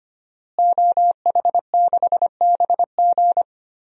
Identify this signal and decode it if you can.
Morse code, heard as a keyed tone, about 25 wpm, 700 Hz. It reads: OH6BG
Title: DXCC-100 Call Signs - 5 Characters - Repeated 2x - 25wpm